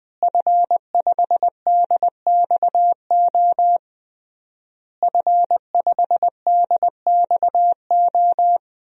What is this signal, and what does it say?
Morse code, heard as a keyed tone, about 21 wpm, 700 Hz. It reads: F5DXO F5DXO
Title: DXCC-100 Call Signs - 5 Characters - Repeated 3x - 20wpm